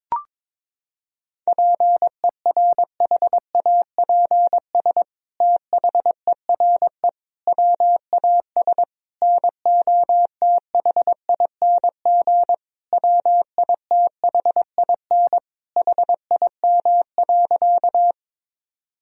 Morse code, heard as a keyed tone, about 22 wpm, 700 Hz. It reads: PERHAPS THERE WAS NOTHING WITHIN HIM.
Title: Sentences from Top 1000 Words - Review Part 4 - 22wpm